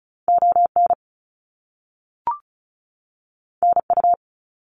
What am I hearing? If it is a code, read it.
Morse code, heard as a keyed tone, about 36 wpm, 700 Hz. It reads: ON NU
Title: Canadian Provinces and Territories 35wpm